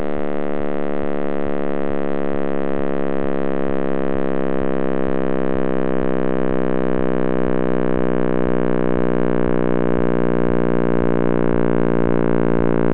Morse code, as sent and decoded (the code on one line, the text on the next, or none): none